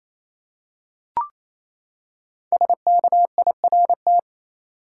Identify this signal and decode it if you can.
Morse code, heard as a keyed tone, about 28 wpm, 700 Hz. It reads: SKIRT
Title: Five Letter Words - 28wpm